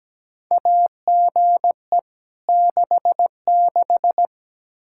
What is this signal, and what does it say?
Morse code, heard as a keyed tone, about 17 wpm, 700 Hz. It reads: AGE 66
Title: K1USN SST QSO Practice 17wpm